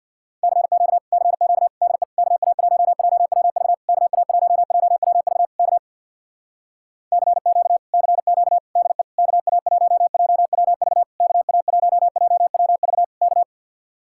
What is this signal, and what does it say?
Morse code, heard as a keyed tone, about 66 wpm, 700 Hz. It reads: CQ CQ DE KA1JWV KA1JWV K CQ CQ DE KA1JWV KA1JWV K